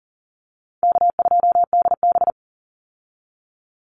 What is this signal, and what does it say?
Morse code, heard as a keyed tone, about 40 wpm, 700 Hz. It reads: K2DB
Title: Easy Call Signs Repeated 3x - 40wpm